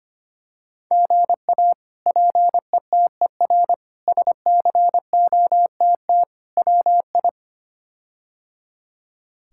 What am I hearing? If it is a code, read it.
Morse code, heard as a keyed tone, about 25 wpm, 700 Hz. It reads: GA PETER SCOTT WI
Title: K1USN SST QSO Practice - 25wpm